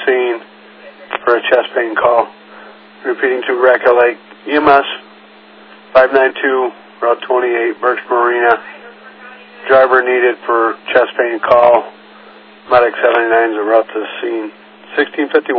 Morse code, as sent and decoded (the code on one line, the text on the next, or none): none